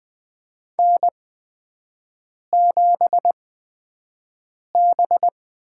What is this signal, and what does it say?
Morse code, heard as a keyed tone, about 20 wpm, spaced as for 5 wpm, 700 Hz. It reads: N7B
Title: USA Call Signs - 5 Characters - 8x Character Spacing - 20wpm